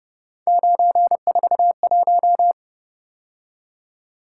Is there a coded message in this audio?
Yes: Morse code from 0.5 to 2.5 s, heard as a keyed tone, about 31 wpm, 700 Hz. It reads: 941